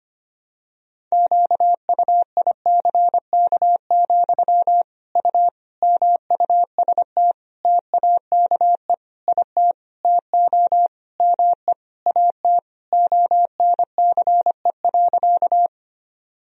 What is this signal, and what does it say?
Morse code, heard as a keyed tone, about 25 wpm, 700 Hz. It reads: QUICK, U MUST TAKE IT TO ME AT ONCE.
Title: Sentences from Top 500 Words and Abbreviations 25wpm